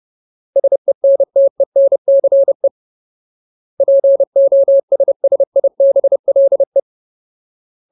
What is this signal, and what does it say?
Morse code, heard as a keyed tone, about 30 wpm, 550 Hz. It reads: SENTENCE POSSIBLE
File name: Ultimate Word List / 30WPM Ultimate